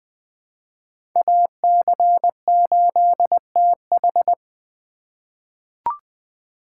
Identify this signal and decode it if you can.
Morse code, heard as a keyed tone, about 20 wpm, 700 Hz. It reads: AC8TH
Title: USA Call Signs - 4x Character Spacing - 20wpm